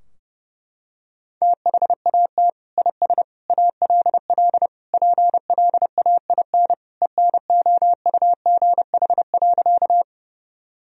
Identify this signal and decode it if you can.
Morse code, heard as a keyed tone, about 32 wpm, 700 Hz. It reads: THAT IS ALL PLAIN ENOUGH.
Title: Sentences from Top 1000 Words - Review Part 2 - 30wpm